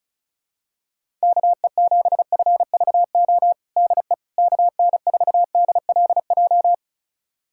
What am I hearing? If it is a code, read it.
Morse code, heard as a keyed tone, about 34 wpm, 700 Hz. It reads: KE7FVO DE KN4DLJ
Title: USA Callsign DE Callsign - 6 Characters - 35wpm